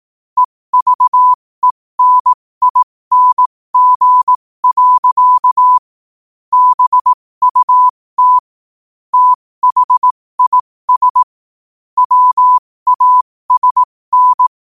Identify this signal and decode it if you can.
Morse code, heard as a keyed tone, about 18 wpm, spaced as for 15 wpm, 1000 Hz. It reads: EVENING. BUT THIS WASN
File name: qotdfiles / 15wpm